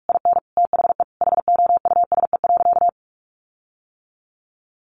Morse code, heard as a keyed tone, about 44 wpm, 700 Hz. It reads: IN THE HOUSE.
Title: Sentences from Top 1300 Words 45wpm